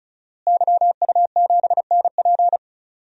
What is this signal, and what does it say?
Morse code, heard as a keyed tone, about 34 wpm, 700 Hz. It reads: YU7NP
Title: DXCC-100 Call Signs - 5 Characters - Repeated 2x - 35wpm